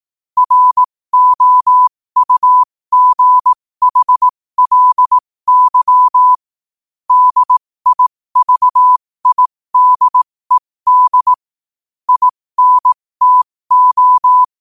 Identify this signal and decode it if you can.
Morse code, heard as a keyed tone, about 18 wpm, spaced as for 15 wpm, 1000 Hz. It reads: ROUGHLY DIVIDED INTO
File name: qotdfiles / 15wpm